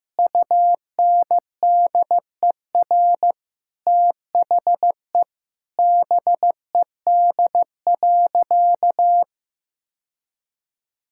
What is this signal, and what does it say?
Morse code, heard as a keyed tone, about 15 wpm, 700 Hz. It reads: UNDER THE BED.
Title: Sentences from Top 1000 Words - Review Part 6 - 15wpm